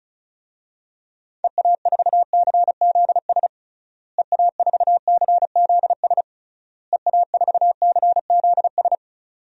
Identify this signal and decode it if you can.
Morse code, heard as a keyed tone, about 34 wpm, 700 Hz. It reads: EA4CZS EA4CZS EA4CZS
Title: DXCC-100 Call Signs - 6 Characters - Repeated 3x - 35wpm